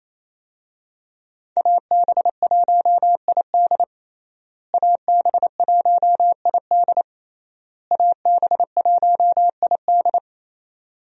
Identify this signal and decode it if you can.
Morse code, heard as a keyed tone, about 27 wpm, 700 Hz. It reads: AB1ID AB1ID AB1ID
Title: USA Call Signs - 5 Characters - Repeated 3x - 28wpm